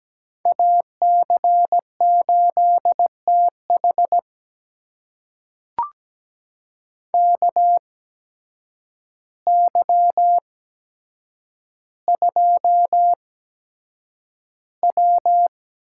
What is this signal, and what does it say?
Morse code, heard as a keyed tone, about 17 wpm, spaced as for 8 wpm, 700 Hz. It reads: AC8TH KY2W
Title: USA Call Signs - 8x Character Spacing - 17wpm